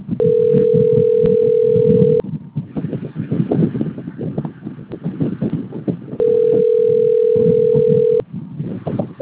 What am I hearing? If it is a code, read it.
Ringback tone (North America), 440 + 480 Hz, with noise.
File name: rg-1600-unknown- - - .26896